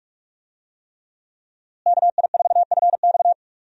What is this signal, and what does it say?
Morse code, heard as a keyed tone, about 42 wpm, 700 Hz. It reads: KI4FX
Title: USA Call Signs - 5 Characters - Repeated 3x - 45wpm